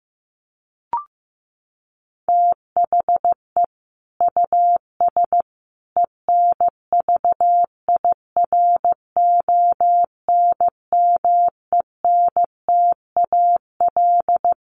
Morse code, heard as a keyed tone, about 15 wpm, 700 Hz. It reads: THE US ENVIRONMENTAL
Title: Sets of 5 Words Encore2 15wpm